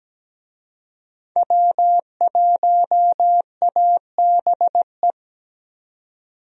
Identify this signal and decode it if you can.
Morse code, heard as a keyed tone, about 17 wpm, 700 Hz. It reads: W1ABE
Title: USA Call Signs - 5 Characters - 17wpm